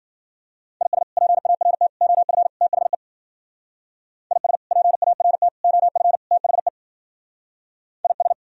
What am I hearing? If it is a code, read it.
Morse code, heard as a keyed tone, about 50 wpm, 700 Hz. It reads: IS PART OF THE IS PART OF THE IS
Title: Speed-Racing - Sets of 4 Words 40wpm